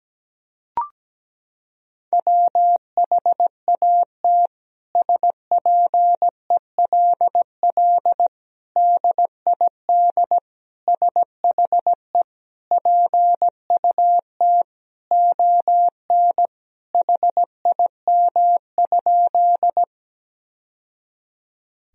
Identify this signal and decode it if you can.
Morse code, heard as a keyed tone, about 17 wpm, 700 Hz. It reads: WHAT SPELL DID SHE PUT ON HIM?
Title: Sentences from Top 1000 Words - Review Part 1 - 17wpm